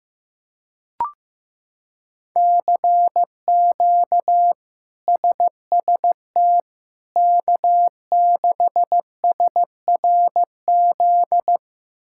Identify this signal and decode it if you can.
Morse code, heard as a keyed tone, about 15 wpm, 700 Hz. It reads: CQ SST K6SRZ
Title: K1USN SST QSO Practice - 15wpm